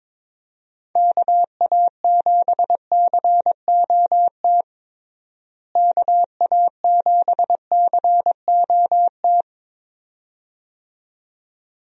Morse code, heard as a keyed tone, about 22 wpm, 700 Hz. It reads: KA7COT KA7COT
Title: USA Call Signs - 6 Characters - Repeated 2x - 22wpm